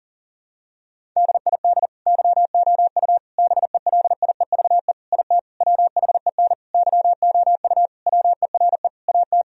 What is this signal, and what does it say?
Morse code, heard as a keyed tone, about 40 wpm, 700 Hz. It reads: DID YOU BELIEVE IT WHEN YOU WERE AT
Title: Sentences from Top 1000 Words - Review Part 1 - 40wpm